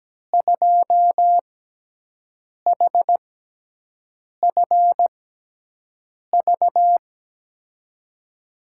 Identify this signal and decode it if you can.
Morse code, heard as a keyed tone, about 17 wpm, spaced as for 6 wpm, 700 Hz. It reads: 2HFV
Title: USA Call Signs - 6 Characters - 6x Character Spacing - 17wpm